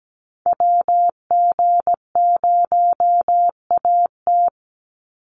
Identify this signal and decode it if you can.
Morse code, heard as a keyed tone, about 17 wpm, 700 Hz. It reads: WG0AT
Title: NA SOTA Activator Callsigns 17wpm